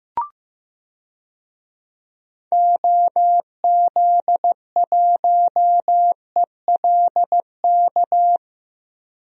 Morse code, heard as a keyed tone, about 15 wpm, 700 Hz. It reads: OZ1ELK
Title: DXCC-100 Call Signs - 6 Characters - Repeated 2x - 15wpm